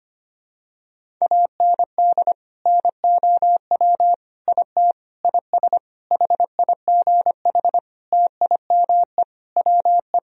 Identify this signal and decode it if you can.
Morse code, heard as a keyed tone, about 25 wpm, 700 Hz. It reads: AND NOW IT IS HIGH TIME WE
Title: Sentences from Top 1000 Words - Review Part 1 - 25wpm